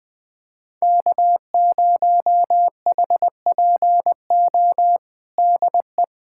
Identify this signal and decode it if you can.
Morse code, heard as a keyed tone, about 21 wpm, 700 Hz. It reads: K0HPO DE
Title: USA Callsign DE Callsign - 5 Characters - 20wpm